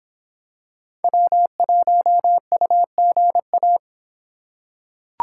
Morse code, heard as a keyed tone, about 23 wpm, 700 Hz. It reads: W1UGA